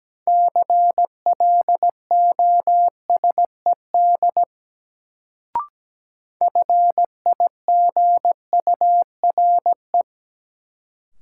Morse code, heard as a keyed tone, about 17 wpm, 700 Hz. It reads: CLOSED FIGURE
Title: Sentences from Top 1400 Words - New Word Drill Alt - 17wpm